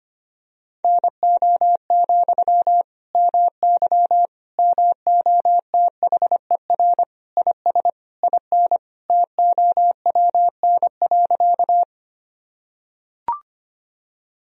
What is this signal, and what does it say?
Morse code, heard as a keyed tone, about 25 wpm, 700 Hz. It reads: NO, MY MOTHER IS IN TOWN.
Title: Sentences from Top 1000 Words - Review Part 2 - 25wpm